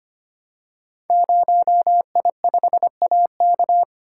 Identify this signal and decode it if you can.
Morse code, heard as a keyed tone, about 25 wpm, 700 Hz. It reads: 0I5AK
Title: Code Groups - 5 Letter:Numbers - 25wpm